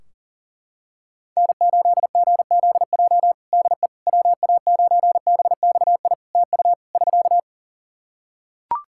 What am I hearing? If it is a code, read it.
Morse code, heard as a keyed tone, about 40 wpm, 700 Hz. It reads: N8GZJ DE WA9BXI TU <SK>
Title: K1USN SST QSO Practice 40wpm